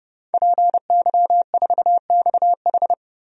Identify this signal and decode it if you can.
Morse code, heard as a keyed tone, about 32 wpm, 700 Hz. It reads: PY4XH
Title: DXCC-100 Call Signs - 5 Characters - Repeated 3x - 30wpm